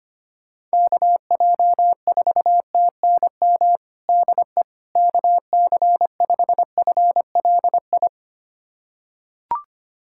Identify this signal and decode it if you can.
Morse code, heard as a keyed tone, about 25 wpm, 700 Hz. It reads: KJ4TNM DE KC5FLI
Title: USA Callsign DE Callsign - 6 Characters - 25wpm